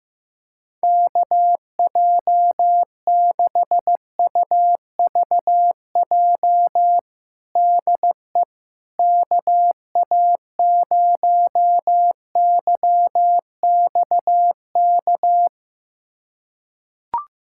Morse code, heard as a keyed tone, about 15 wpm, 700 Hz. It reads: KJ6UVJ DE KA0YXK
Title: USA Callsign DE Callsign - 6 Characters - 15wpm